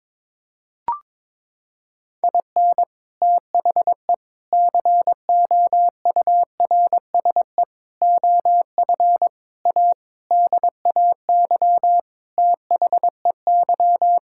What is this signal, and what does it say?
Morse code, heard as a keyed tone, about 22 wpm, 700 Hz. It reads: IN THE COURSE OF A DAY THEY
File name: practice / 22wpm / Sentences with 4-Word Phrases - 22wpm